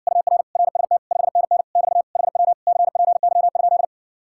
Morse code, heard as a keyed tone, about 60 wpm, 690 Hz. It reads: UR RST 5NN <BT> HW CPY?